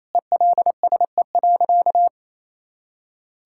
Morse code, heard as a keyed tone, about 28 wpm, 700 Hz. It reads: ELSE.
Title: Sentences from Top 1600 Words 28wpm